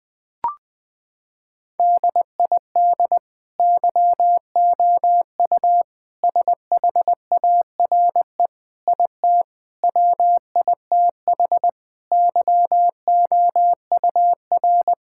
Morse code, heard as a keyed tone, about 20 wpm, 700 Hz. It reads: DID YOU SHARE IT WITH YOUR
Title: Sentences from Top 1000 Words - Review Part 6 - 20wpm